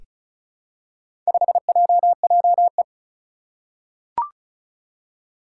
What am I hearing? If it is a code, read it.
Morse code, heard as a keyed tone, about 34 wpm, 700 Hz. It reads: 5JJE
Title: Four Challenging Letter-Number Combinations 35wpm